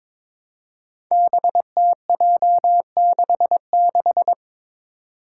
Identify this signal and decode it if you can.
Morse code, heard as a keyed tone, about 22 wpm, 700 Hz. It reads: BTJ66
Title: Five Challenging Letter-Number Combinations 22wpm